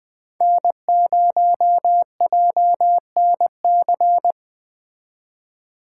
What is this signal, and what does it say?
Morse code, heard as a keyed tone, about 20 wpm, 700 Hz. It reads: N0JNC
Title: Speed Racing - USA Call Signs 17wpm